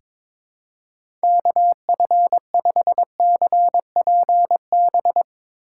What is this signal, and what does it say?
Morse code, heard as a keyed tone, about 22 wpm, 700 Hz. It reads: KF5CPB
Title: USA Call Signs - 6 Characters - 2x Character Spacing - 22wpm